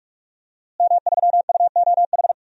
Morse code, heard as a keyed tone, about 46 wpm, 700 Hz. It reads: M3UOH
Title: DXCC-100 Call Signs - 5 Characters - Repeated 2x - 45wpm